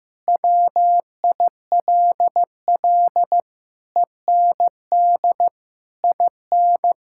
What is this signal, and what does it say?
Morse code, heard as a keyed tone, about 15 wpm, 700 Hz. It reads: WILL END IN
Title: Sentences from Top 1000 Words - Review Part 5 - 15wpm